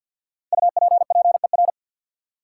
Morse code, heard as a keyed tone, about 50 wpm, 700 Hz. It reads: UPPER
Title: Five Letter Words - 50wpm